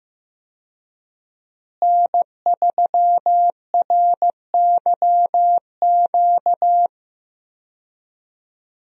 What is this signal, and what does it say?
Morse code, heard as a keyed tone, about 15 wpm, 700 Hz. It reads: N3RYQ
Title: USA Call Signs - 5 Characters - Repeated 2x - 15wpm